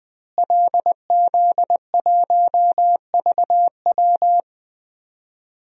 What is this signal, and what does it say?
Morse code, heard as a keyed tone, about 21 wpm, 700 Hz. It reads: LZ1VW
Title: DXCC-100 Call Signs - 5 Characters - Repeated 3x - 20wpm